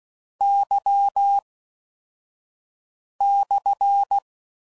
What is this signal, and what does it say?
Morse code, heard as a keyed tone, about 16 wpm, spaced as for 4 wpm, 800 Hz. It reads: Y/